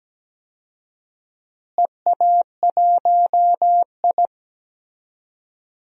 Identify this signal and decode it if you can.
Morse code, heard as a keyed tone, about 17 wpm, 700 Hz. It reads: EA1I
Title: DXCC-100 Call Signs - 4 Characters - Repeated 3x - 17wpm